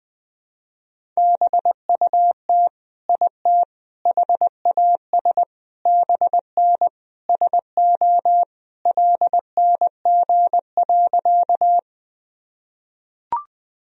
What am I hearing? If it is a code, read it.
Morse code, heard as a keyed tone, about 20 wpm, 700 Hz. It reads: BUT IT HAS BN SO LNG.
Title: Sentences from Top 100 Words and Abbreviations 20wpm